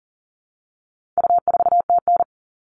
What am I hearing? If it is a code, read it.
Morse code, heard as a keyed tone, about 40 wpm, 700 Hz. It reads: U4TN
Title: All Cut Numbers - 4-Digit 40wpm